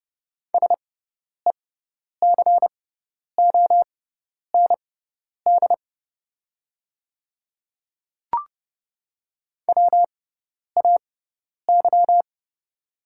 Morse code, heard as a keyed tone, about 30 wpm, spaced as for 10 wpm, 690 Hz. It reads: SECOND WAY
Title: 6x Character Spacing - Top 300 Words - 30wpm